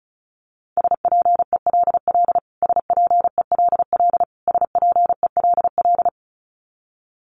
Morse code, heard as a keyed tone, about 34 wpm, 700 Hz. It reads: SPELL SPELL SPELL
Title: Top 200 Words Repeated 35wpm